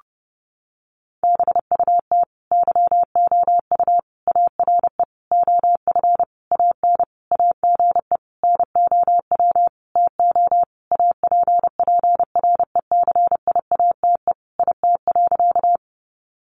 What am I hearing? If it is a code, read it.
Morse code, heard as a keyed tone, about 30 wpm, 700 Hz. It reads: BUT YOU ARE OF AN AGE NOW TO APPRECIATE IT.
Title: Sentences from Top 1300 Words 30wpm